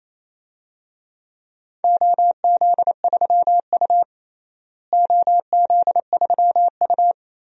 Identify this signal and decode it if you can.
Morse code, heard as a keyed tone, about 28 wpm, 700 Hz. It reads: OZ3U OZ3U
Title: DXCC-100 Call Signs - 4 Characters - Repeated 3x - 28wpm